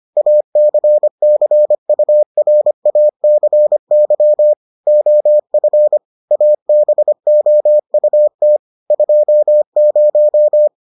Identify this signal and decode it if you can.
Morse code, heard as a keyed tone, about 25 wpm, 600 Hz. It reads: ACCURACY OF ABOUT 20